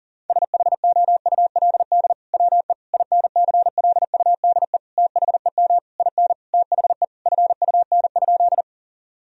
Morse code, heard as a keyed tone, about 40 wpm, 700 Hz. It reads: SHOULD WE INCLUDE THEM IN THE FUN?
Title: Sentences from Top 1000 Words - Review Part 3 - 40wpm